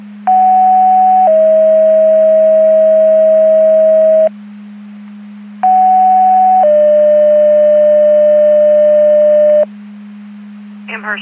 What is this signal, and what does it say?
Two-tone page, heard as a one-second tone, then a three-second tone, about 750 then 640 Hz, about 750 then 600 Hz. It is heard from 0.3 to 9.6 s.
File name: Portage / 01181 10-24 Portage 154.385 Paging